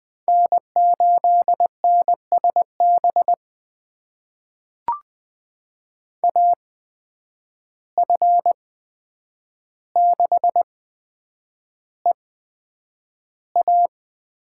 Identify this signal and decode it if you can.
Morse code, heard as a keyed tone, about 20 wpm, spaced as for 9 wpm, 700 Hz. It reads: N8NSB AF6EA
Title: USA Call Signs - 5 Characters - 8x Character Spacing - 20wpm